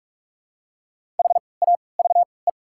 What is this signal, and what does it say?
Morse code, heard as a keyed tone, about 46 wpm, spaced as for 26 wpm, 700 Hz. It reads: HAVE
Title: 3x Character Spacing - Top 300 Words - 45wpm